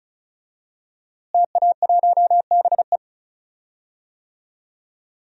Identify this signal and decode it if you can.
Morse code, heard as a keyed tone, about 36 wpm, 700 Hz. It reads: TA1BE
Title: DXCC-100 Call Signs - 5 Characters - Repeated 3x - 35wpm